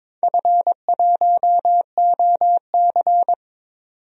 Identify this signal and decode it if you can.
Morse code, heard as a keyed tone, about 22 wpm, 700 Hz. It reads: F1OC
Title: DXCC-100 Call Signs - 4 Characters - 22wpm